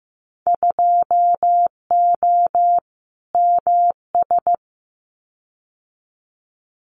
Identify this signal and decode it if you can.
Morse code, heard as a keyed tone, about 15 wpm, 700 Hz. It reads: 2O MS
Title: Winter Field Day Exchanges 15wpm